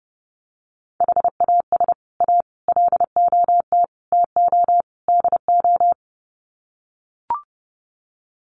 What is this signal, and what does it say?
Morse code, heard as a keyed tone, about 30 wpm, 700 Hz. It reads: HAS A LOT TO DO
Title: Sets of 5 Words Encore2 30wpm